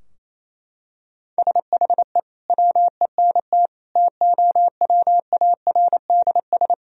Morse code, heard as a keyed tone, about 28 wpm, 700 Hz. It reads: SHE WENT TOWARDS
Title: Sentences from Top 1000 Words - Review Part 2 - 28wpm